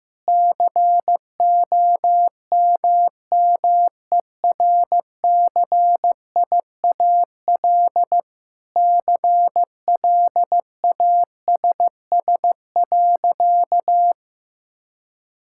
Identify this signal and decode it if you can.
Morse code, heard as a keyed tone, about 15 wpm, 700 Hz. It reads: COMMERCIAL CLASS.
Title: Sentences from Top 1000 Words - Review Part 5 - 15wpm